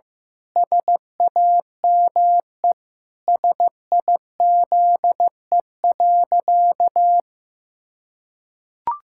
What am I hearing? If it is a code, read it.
Morse code, heard as a keyed tone, about 15 wpm, 700 Hz. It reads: SAME SIZE.
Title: Pangram Challenge v2 - 15wpm